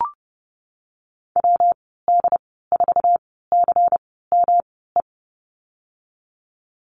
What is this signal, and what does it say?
Morse code, heard as a keyed tone, about 30 wpm, spaced as for 17 wpm, 700 Hz. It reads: WD4CME
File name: Lesson 19g - Letter w - Call Signs 30wpm